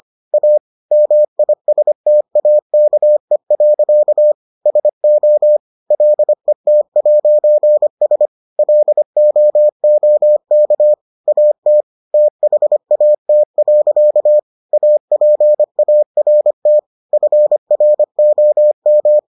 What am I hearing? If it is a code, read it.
Morse code, heard as a keyed tone, about 25 wpm, 600 Hz. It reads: A MISTAKE. SO LET'S LOOK AT THAT. APART FROM